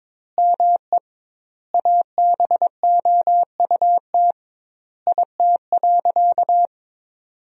Morse code, heard as a keyed tone, about 22 wpm, 700 Hz. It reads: ME ABOUT IT.
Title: 2x Word-Spacing - Sentences from Top 200 Words - 22wpm